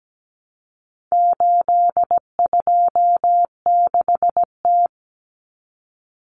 Morse code, heard as a keyed tone, about 17 wpm, 700 Hz. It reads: 826T
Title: Common Cut Numbers - 4-Digit 17wpm